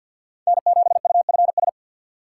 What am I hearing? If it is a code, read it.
Morse code, heard as a keyed tone, about 53 wpm, 700 Hz. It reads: N7UFS
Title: USA Call Signs - 5 Characters - 3x Character Spacing - 50wpm